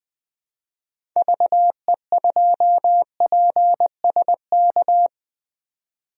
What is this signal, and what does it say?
Morse code, heard as a keyed tone, about 20 wpm, 700 Hz. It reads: VE2PSK
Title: DXCC-100 Call Signs - 6 Characters - 20wpm